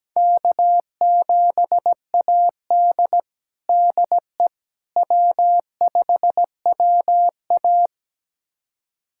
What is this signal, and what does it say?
Morse code, heard as a keyed tone, about 17 wpm, 700 Hz. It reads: K7AD DE W5WA
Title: USA Callsign DE Callsign - 4 Characters - 17wpm